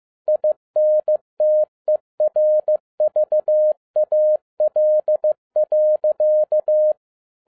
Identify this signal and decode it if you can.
Morse code, heard as a keyed tone, about 15 wpm, 610 Hz. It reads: INTERVAL.